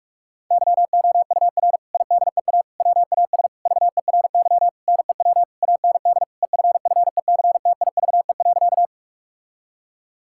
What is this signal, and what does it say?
Morse code, heard as a keyed tone, about 46 wpm, 700 Hz. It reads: YOUR IDEA WAS VERY NEW AND EFFECTIVE.
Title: Sentences from Top 1000 Words - Review Part 6 - 45wpm